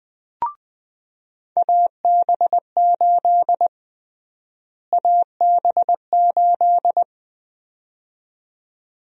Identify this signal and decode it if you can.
Morse code, heard as a keyed tone, about 20 wpm, 700 Hz. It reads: AB8 AB8
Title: USA Call Sign Prefixes - Repeated 2x - 20wpm